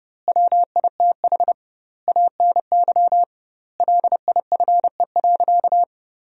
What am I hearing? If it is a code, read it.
Morse code, heard as a keyed tone, about 30 wpm, 700 Hz. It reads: WITH ANY LIFE.
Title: 2x Word-Spacing - Sentences from Top 300 Words - 30wpm